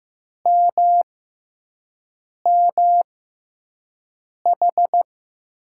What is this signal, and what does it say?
Morse code, heard as a keyed tone, about 15 wpm, spaced as for 5 wpm, 700 Hz. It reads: MMH
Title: USA Call Signs - 6 Characters - 6x Character Spacing - 15wpm